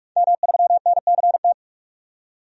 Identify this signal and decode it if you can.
Morse code, heard as a keyed tone, about 42 wpm, 700 Hz. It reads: M3NCT
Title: DXCC-100 Call Signs - 5 Characters - 45wpm